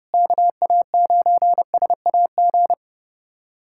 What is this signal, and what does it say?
Morse code, heard as a keyed tone, about 30 wpm, 700 Hz. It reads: KA9SAG
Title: USA Callsign DE Callsign - 6 Characters - 30wpm